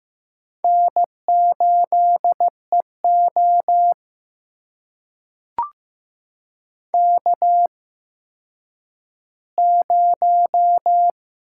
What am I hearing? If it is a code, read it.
Morse code, heard as a keyed tone, about 15 wpm, spaced as for 9 wpm, 700 Hz. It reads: N8EO K0
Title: USA Call Signs - 4 Characters - 8x Character Spacing - 15wpm